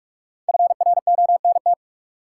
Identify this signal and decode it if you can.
Morse code, heard as a keyed tone, about 47 wpm, 700 Hz. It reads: FRONT
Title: Sentences from Top 500 Words - New Word Drill - 45wpm